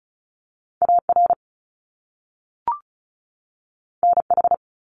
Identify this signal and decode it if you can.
Morse code, heard as a keyed tone, about 36 wpm, 700 Hz. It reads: AR NH
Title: US States 35wpm